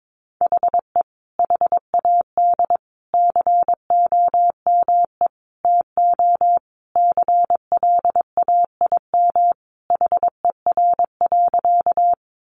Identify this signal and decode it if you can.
Morse code, heard as a keyed tone, about 22 wpm, 700 Hz. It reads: HE HAD COME TO CLAIM HER.
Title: Sentences from Top 1500 Words 22wpm